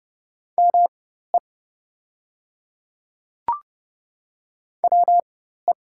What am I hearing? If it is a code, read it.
Morse code, heard as a keyed tone, about 30 wpm, spaced as for 14 wpm, 690 Hz. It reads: ME WE
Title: 4x Character Spacing - Top 200 Words - 30wpm